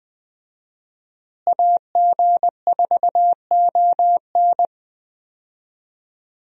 Morse code, heard as a keyed tone, about 20 wpm, 700 Hz. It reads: AG4ON
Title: USA Call Signs - 5 Characters - Repeated 2x - 20wpm